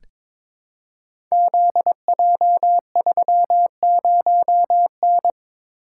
Morse code, heard as a keyed tone, about 22 wpm, 700 Hz. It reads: ZJ30N
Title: Code Groups - 5 Letter:Numbers - 22wpm